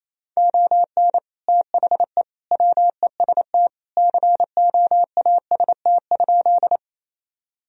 Morse code, heard as a keyed tone, about 28 wpm, 700 Hz. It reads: ON THE WEST COAST?
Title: Sentences from Top 1000 Words - Review Part 4 - 28wpm